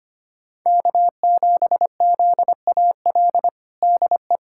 Morse code, heard as a keyed tone, about 25 wpm, 700 Hz. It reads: K7ZAL DE
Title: USA Callsign DE Callsign - 5 Characters - 25wpm